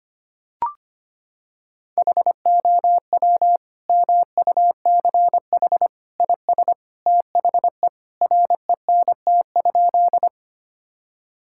Morse code, heard as a keyed tone, about 25 wpm, 700 Hz. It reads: HOW MUCH IS THE RENT?
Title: Sentences from Top 1000 Words - Review Part 5 - 25wpm